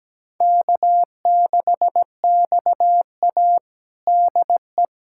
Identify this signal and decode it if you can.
Morse code, heard as a keyed tone, about 17 wpm, 700 Hz. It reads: K6XA DE
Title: USA Callsign DE Callsign - 4 Characters - 17wpm